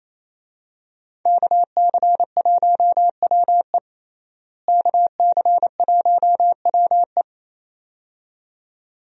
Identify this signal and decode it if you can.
Morse code, heard as a keyed tone, about 28 wpm, 700 Hz. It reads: KC1WE KC1WE
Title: USA Call Signs - 5 Characters - Repeated 2x - 28wpm